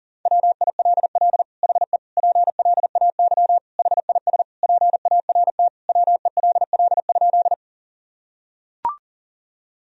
Morse code, heard as a keyed tone, about 40 wpm, 700 Hz. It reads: WILL HE PLAY HIS PART WELL?
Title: Sentences from Top 1000 Words - Review Part 1 - 40wpm